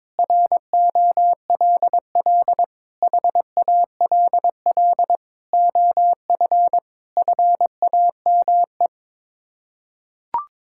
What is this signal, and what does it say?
Morse code, heard as a keyed tone, about 22 wpm, 700 Hz. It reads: ROLL HALL OF FAME
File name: practice / 22wpm / Sets of 7 Words - 22wpm